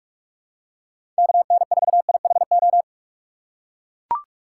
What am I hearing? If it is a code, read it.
Morse code, heard as a keyed tone, about 44 wpm, 700 Hz. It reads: KN4IHO
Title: USA Call Signs - 6 Characters - 2x Character Spacing - 45wpm